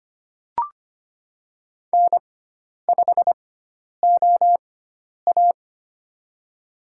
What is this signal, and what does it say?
Morse code, heard as a keyed tone, about 25 wpm, spaced as for 10 wpm, 700 Hz. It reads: N5OA
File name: Lesson 16g - Letter c - Call Signs 25wpm